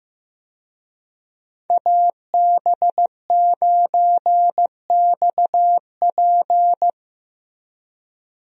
Morse code, heard as a keyed tone, about 16 wpm, 700 Hz. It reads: AB9XP
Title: USA Call Signs - 5 Characters - Repeated 2x - 15wpm